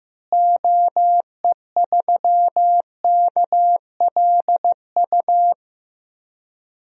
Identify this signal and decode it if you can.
Morse code, heard as a keyed tone, about 15 wpm, 700 Hz. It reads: OE3KLU
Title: DXCC-100 Call Signs - 6 Characters - Repeated 3x - 15wpm